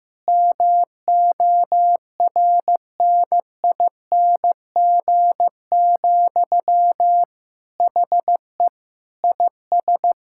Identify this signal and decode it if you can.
Morse code, heard as a keyed tone, about 15 wpm, 700 Hz. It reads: MORNING, HE IS
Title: Sentences from Top 1000 Words - Review Part 6 - 15wpm